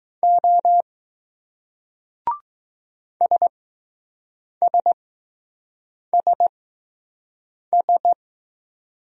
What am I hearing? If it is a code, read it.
Morse code, heard as a keyed tone, about 20 wpm, spaced as for 6 wpm, 700 Hz. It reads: O SSSS